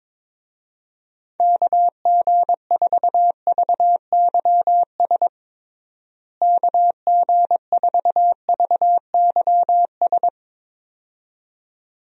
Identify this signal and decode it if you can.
Morse code, heard as a keyed tone, about 22 wpm, 700 Hz. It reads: KG4VYS KG4VYS
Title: USA Call Signs - 6 Characters - Repeated 2x - 22wpm